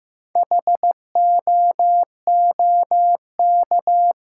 Morse code, heard as a keyed tone, about 15 wpm, 700 Hz. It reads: HOOK
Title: Sentences from Top 1600 Words - New Word Drill Alt - 15wpm